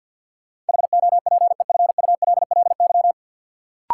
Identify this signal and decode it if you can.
Morse code, heard as a keyed tone, about 53 wpm, 700 Hz. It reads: HOPEFULLY
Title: Sentences from Top 1500 Words - New Word Drill Alt - 50wpm